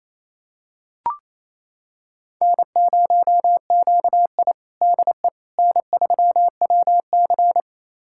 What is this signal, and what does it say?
Morse code, heard as a keyed tone, about 28 wpm, 700 Hz. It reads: N0QI DE N3WC
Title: USA Callsign DE Callsign - 4 Characters - 28wpm